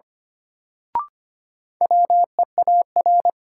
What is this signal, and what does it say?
Morse code, heard as a keyed tone, about 25 wpm, 700 Hz. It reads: WEAR
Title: Sentences from Top 1500 Words - New Word Drill Alt - 25wpm